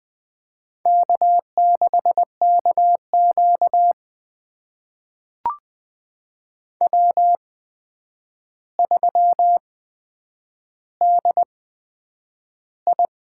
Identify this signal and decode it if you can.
Morse code, heard as a keyed tone, about 20 wpm, spaced as for 9 wpm, 700 Hz. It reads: K6KQ W3DI
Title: USA Call Signs - 8x Character Spacing - 20wpm